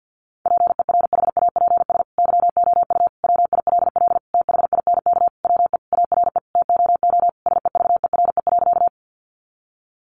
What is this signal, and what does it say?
Morse code, heard as a keyed tone, about 50 wpm, 700 Hz. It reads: PERHAPS YOU WILL THINK WE ARE TOO SEVERE.
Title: Sentences from Top 1400 Words 50wpm